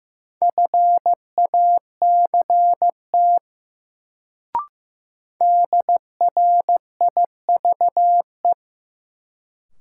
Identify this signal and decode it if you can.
Morse code, heard as a keyed tone, about 15 wpm, 700 Hz. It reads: FACT DRIVE
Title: Sentences from Top 500 Words - New Word Drill - 15wpm